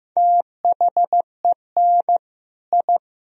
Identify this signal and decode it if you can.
Morse code, heard as a keyed tone, about 16 wpm, 700 Hz. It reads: THEN I
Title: Sentences from Top 1000 Words - Review Part 1 - 15wpm